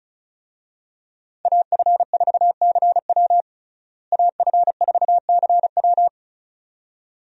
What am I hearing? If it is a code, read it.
Morse code, heard as a keyed tone, about 36 wpm, 700 Hz. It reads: AF4CW AF4CW
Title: USA Call Signs - 5 Characters - Repeated 2x - 35wpm